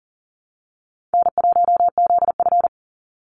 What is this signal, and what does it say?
Morse code, heard as a keyed tone, about 37 wpm, 700 Hz. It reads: N1ZF
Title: NA SOTA Chaser Callsigns 40wpm